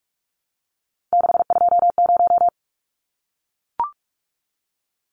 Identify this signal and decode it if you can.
Morse code, heard as a keyed tone, about 44 wpm, 700 Hz. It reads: 620
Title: Numbers - 3-Digit 45wpm